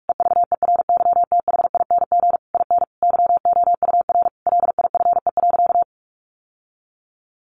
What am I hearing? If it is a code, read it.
Morse code, heard as a keyed tone, about 44 wpm, 700 Hz. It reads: EVERYTHING IN YOUR LIFE.
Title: Sentences from Top 700 Words 45wpm